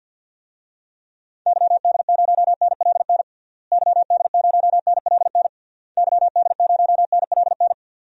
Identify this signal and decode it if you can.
Morse code, heard as a keyed tone, about 53 wpm, 700 Hz. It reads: YD0NLN YD0NLN YD0NLN
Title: DXCC-100 Call Signs - 6 Characters - Repeated 3x - 50wpm